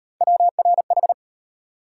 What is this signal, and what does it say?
Morse code, heard as a keyed tone, about 36 wpm, 700 Hz. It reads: WRH